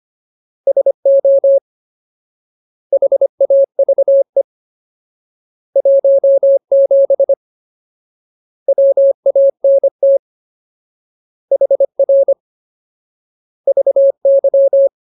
Morse code, heard as a keyed tone, about 25 wpm, 550 Hz. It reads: SO HAVE 17 WANT HR VY
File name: Ultimate Word List / 25WPM Ultimate